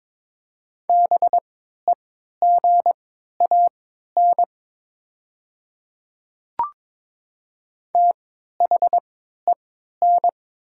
Morse code, heard as a keyed tone, about 22 wpm, spaced as for 13 wpm, 700 Hz. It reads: BEGAN THEN
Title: 3x Character Spacing - Top 300 Words - 22wpm